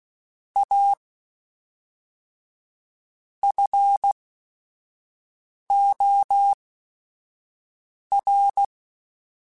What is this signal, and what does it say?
Morse code, heard as a keyed tone, about 16 wpm, spaced as for 5 wpm, 800 Hz. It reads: A FOR
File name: Technical Subjects / Technical Subjects-mp3s / XL @5